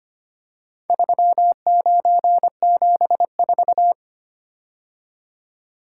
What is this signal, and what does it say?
Morse code, heard as a keyed tone, about 25 wpm, 700 Hz. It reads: 3974
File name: practice / 25wpm / Four Letter-Number Combinations - 25wpm